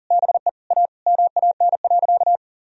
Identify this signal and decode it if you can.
Morse code, heard as a keyed tone, about 37 wpm, 700 Hz. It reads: BE A MAN.
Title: Sentences from Top 1000 Words - Review Part 4 - 40wpm